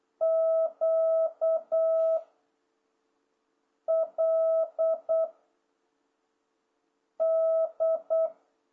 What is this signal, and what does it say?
Morse code, heard as a keyed tone, about 8 wpm, 660 Hz. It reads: Q L D